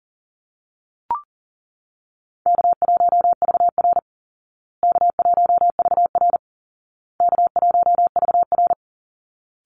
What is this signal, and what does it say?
Morse code, heard as a keyed tone, about 40 wpm, 700 Hz. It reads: K1VR K1VR K1VR
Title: Call Signs Repeated 2x - 40wpm